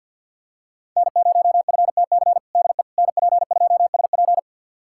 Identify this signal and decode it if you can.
Morse code, heard as a keyed tone, about 51 wpm, 700 Hz. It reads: N0FTC DE NP2SP
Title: USA Callsign DE Callsign - 5 Characters - 50wpm